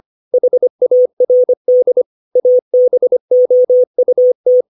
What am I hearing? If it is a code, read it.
Morse code, heard as a keyed tone, about 25 wpm, 500 Hz. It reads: HARD ABOUT
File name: 632 words / 632 Words 25WPM